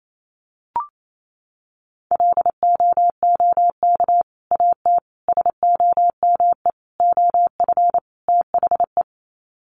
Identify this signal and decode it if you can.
Morse code, heard as a keyed tone, about 28 wpm, 700 Hz. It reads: LOOK AT SOME OF THE
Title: Sets of 5 Words Encore 28wpm